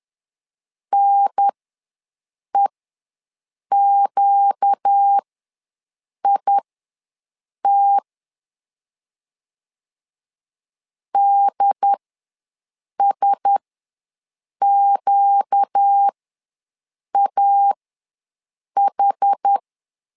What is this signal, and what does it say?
Morse code, heard as a keyed tone, about 10 wpm, spaced as for 6 wpm, 800 Hz. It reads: NEQIT DSQAH
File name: LEZIONI / 4 Q1S ws 10wpm 2208